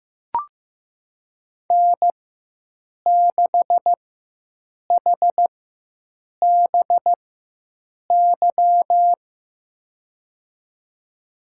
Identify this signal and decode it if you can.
Morse code, heard as a keyed tone, about 15 wpm, spaced as for 7 wpm, 700 Hz. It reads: N6HBY
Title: USA Call Signs - 4x Character Spacing - 15wpm